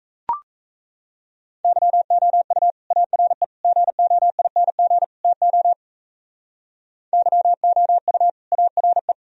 Morse code, heard as a keyed tone, about 35 wpm, 700 Hz. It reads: YOU ARE GOING TO YOU ARE
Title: Speed-Racing - Sets of 4 Words 28wpm